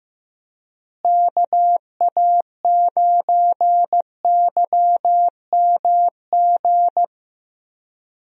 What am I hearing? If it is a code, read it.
Morse code, heard as a keyed tone, about 15 wpm, 700 Hz. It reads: KA9YMG
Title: USA Call Signs - 6 Characters - Repeated 3x - 15wpm